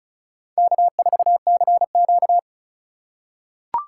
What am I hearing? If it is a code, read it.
Morse code, heard as a keyed tone, about 36 wpm, 700 Hz. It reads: K4CQ
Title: USA Call Signs - 4 Characters - 3x Character Spacing - 35wpm